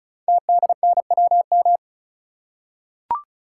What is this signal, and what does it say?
Morse code, heard as a keyed tone, about 36 wpm, 700 Hz. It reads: TDNWM
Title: Code Groups - 5 Letters - 35wpm